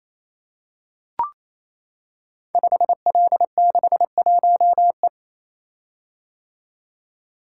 Morse code, heard as a keyed tone, about 27 wpm, 700 Hz. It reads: 5L61E
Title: Code Groups - 5 Letter:Numbers - 28wpm